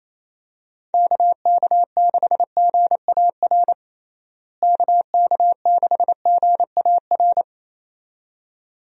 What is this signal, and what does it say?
Morse code, heard as a keyed tone, about 28 wpm, 700 Hz. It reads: KK6GAR KK6GAR
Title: USA Call Signs - 6 Characters - Repeated 2x - 28wpm